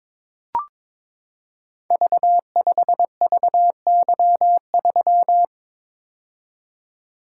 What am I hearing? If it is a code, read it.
Morse code, heard as a keyed tone, about 21 wpm, 700 Hz. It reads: V5VY3
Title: Code Groups - 5 Letter:Numbers - 22wpm